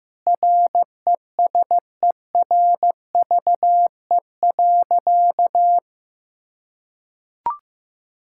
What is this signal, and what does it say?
Morse code, heard as a keyed tone, about 16 wpm, 700 Hz. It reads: RESERVE.
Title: Sentences from Top 1000 Words - Review Part 6 - 15wpm